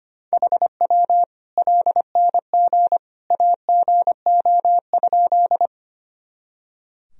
Morse code, heard as a keyed tone, about 25 wpm, 700 Hz. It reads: HW LNG AGO?
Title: Sentences from Top 500 Words and Abbreviations 25wpm